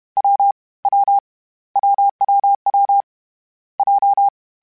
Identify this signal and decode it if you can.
Morse code, heard as a keyed tone, about 31 wpm, 800 Hz. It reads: W W WWW J
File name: Difficult Combos / Difficult Combos-mp3s / AWJ1 @30